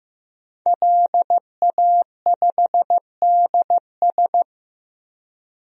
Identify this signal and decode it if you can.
Morse code, heard as a keyed tone, about 15 wpm, 700 Hz. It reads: LA5DS
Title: DXCC-100 Call Signs - 5 Characters - Repeated 3x - 15wpm